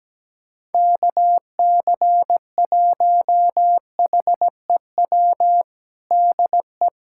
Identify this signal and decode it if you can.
Morse code, heard as a keyed tone, about 17 wpm, 700 Hz. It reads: KC1HEW DE
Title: USA Callsign DE Callsign - 6 Characters - 17wpm